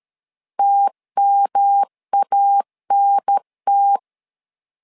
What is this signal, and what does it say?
Morse code, heard as a keyed tone, about 13 wpm, 800 Hz. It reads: TMANT
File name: LEZIONI / 3 MWRO / 3 M 12wpm 1541